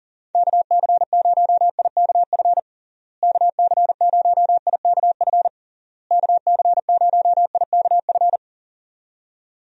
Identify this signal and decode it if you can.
Morse code, heard as a keyed tone, about 40 wpm, 700 Hz. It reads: KC0IKF KC0IKF KC0IKF
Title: USA Call Signs - 6 Characters - Repeated 3x - 40wpm